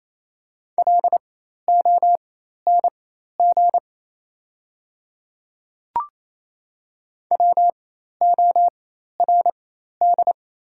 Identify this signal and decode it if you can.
Morse code, heard as a keyed tone, about 28 wpm, spaced as for 13 wpm, 700 Hz. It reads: LONG WORD
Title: 4x Character Spacing - Top 200 Words - 28wpm